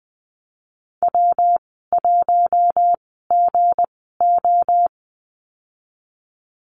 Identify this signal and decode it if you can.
Morse code, heard as a keyed tone, about 20 wpm, spaced as for 14 wpm, 700 Hz. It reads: W1GO
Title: 167 Lesson 27g - Number 7 - Call Signs 20wpm